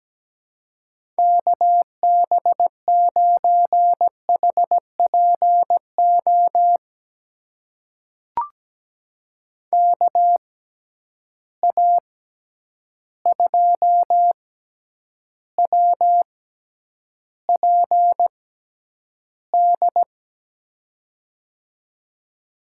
Morse code, heard as a keyed tone, about 17 wpm, spaced as for 9 wpm, 700 Hz. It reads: KB9HPO KA2WPD
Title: USA Call Signs - 6x Character Spacing - 17wpm